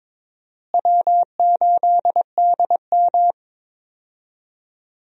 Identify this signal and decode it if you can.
Morse code, heard as a keyed tone, about 22 wpm, 700 Hz. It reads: W8DM
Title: USA Call Signs - 4 Characters - Repeated 3x - 22wpm